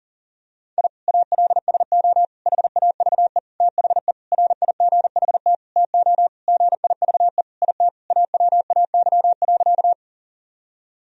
Morse code, heard as a keyed tone, about 40 wpm, 700 Hz. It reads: I ALSO HAVE THE RIGHT TO GIVE IT AWAY.
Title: Sentences from Top 1000 Words - Review Part 1 - 40wpm